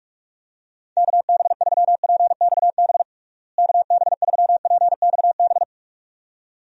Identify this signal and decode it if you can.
Morse code, heard as a keyed tone, about 41 wpm, 700 Hz. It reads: KB3PXB KB3PXB
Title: USA Call Signs - 6 Characters - Repeated 2x - 45wpm